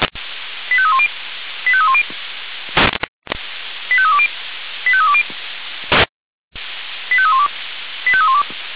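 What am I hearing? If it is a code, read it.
Five-tone selective call (ZVEI), heard as quick runs of five tones, tones of about 70 ms. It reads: 85310 85310 85320 85320 85312 85312
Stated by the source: Received on SDR.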